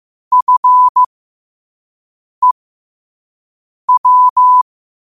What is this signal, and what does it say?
Morse code, heard as a keyed tone, about 15 wpm, spaced as for 5 wpm, 1000 Hz. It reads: FEW
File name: qotdfiles / 5wpm